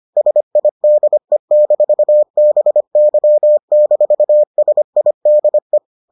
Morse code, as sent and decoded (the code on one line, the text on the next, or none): ... .. -.. -....- -... -.-- -....- ... .. -.. .
SIDE-BY-SIDE